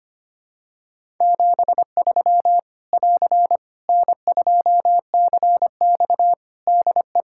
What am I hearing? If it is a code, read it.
Morse code, heard as a keyed tone, about 25 wpm, 700 Hz. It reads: 73 <AR> N2CX DE
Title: Practice QSO - Exchange 4 - Ending - 25wpm